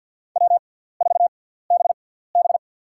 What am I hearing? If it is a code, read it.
Morse code, heard as a keyed tone, about 50 wpm, spaced as for 17 wpm, 700 Hz. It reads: W4BB
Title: USA Call Signs - 6x Character Spacing - 50wpm